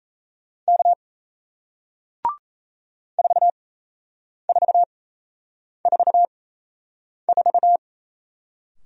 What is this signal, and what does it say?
Morse code, heard as a keyed tone, about 38 wpm, spaced as for 8 wpm, 700 Hz. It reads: K 4444